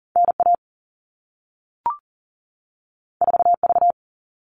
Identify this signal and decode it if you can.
Morse code, heard as a keyed tone, about 40 wpm, 700 Hz. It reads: NA 4V
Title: All Cut Numbers - 2-Digit 40wpm